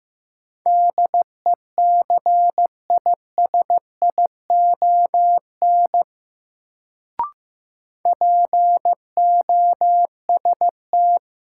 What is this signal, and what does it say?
Morse code, heard as a keyed tone, about 15 wpm, 700 Hz. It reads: DECISION POST
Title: Sentences from Top 1000 Words - New Word Drill - 15wpm